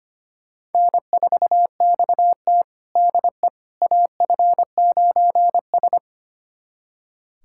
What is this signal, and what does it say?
Morse code, heard as a keyed tone, about 25 wpm, 700 Hz. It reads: N4XT DE AF9S
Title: USA Callsign DE Callsign - 4 Characters - 25wpm